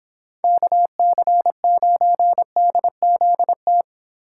Morse code, heard as a keyed tone, about 29 wpm, 700 Hz. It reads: KC9DZT